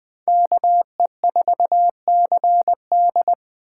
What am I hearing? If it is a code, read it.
Morse code, heard as a keyed tone, about 20 wpm, 700 Hz. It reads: KE4CD
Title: USA Call Signs - 5 Characters - 8x Character Spacing - 20wpm